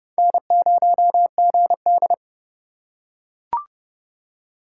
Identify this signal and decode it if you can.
Morse code, heard as a keyed tone, about 28 wpm, 700 Hz. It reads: N0GD